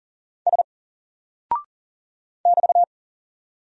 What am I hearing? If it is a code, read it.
Morse code, heard as a keyed tone, about 40 wpm, 700 Hz. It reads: S <BT>
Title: Single Character (Letter:Number:Punctuation:Prosign) 40wpm